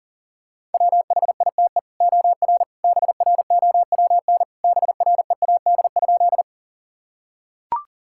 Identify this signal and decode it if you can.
Morse code, heard as a keyed tone, about 40 wpm, 700 Hz. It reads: WHITE OR BROWN BREAD?
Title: Sentences from Top 1600 Words 40wpm